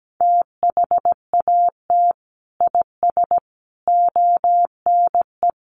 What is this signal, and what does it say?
Morse code, heard as a keyed tone, about 17 wpm, 700 Hz. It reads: THAT IS ONE
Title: Sentences from Top 1200 Words 17wpm